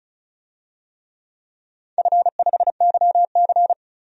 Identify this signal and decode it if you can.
Morse code, heard as a keyed tone, about 36 wpm, 700 Hz. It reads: F5YC
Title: DXCC-100 Call Signs - 4 Characters - 35wpm